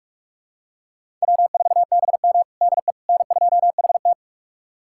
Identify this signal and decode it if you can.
Morse code, heard as a keyed tone, about 46 wpm, 700 Hz. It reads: W4BM DE N2HT
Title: USA Callsign DE Callsign - 4 Characters - 45wpm